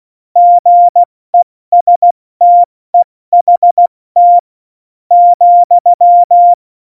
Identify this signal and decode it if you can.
Morse code, heard as a keyed tone, about 16 wpm, spaced as for 13 wpm, 700 Hz. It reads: GESTEHT,
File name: Morsekurs / Witze / Witz 02. Juli 2019, 70 F80